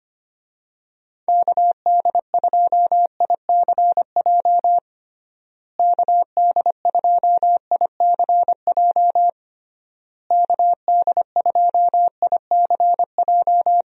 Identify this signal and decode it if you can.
Morse code, heard as a keyed tone, about 25 wpm, 700 Hz. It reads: KD2ICJ KD2ICJ KD2ICJ
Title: USA Call Signs - 6 Characters - Repeated 3x - 25wpm